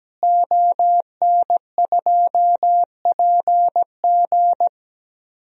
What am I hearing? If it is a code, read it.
Morse code, heard as a keyed tone, about 17 wpm, 700 Hz. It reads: ON2PG
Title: DXCC-100 Call Signs - 5 Characters - Repeated 3x - 17wpm